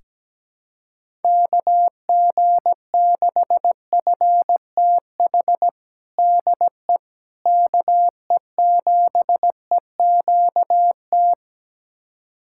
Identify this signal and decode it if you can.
Morse code, heard as a keyed tone, about 17 wpm, 700 Hz. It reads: KG6FTH DE KE7EQT
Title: USA Callsign DE Callsign - 6 Characters - 17wpm